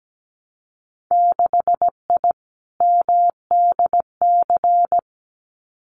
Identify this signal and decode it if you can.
Morse code, heard as a keyed tone, about 17 wpm, 700 Hz. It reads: 6I MDC
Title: Winter Field Day Exchanges Encore 17wpm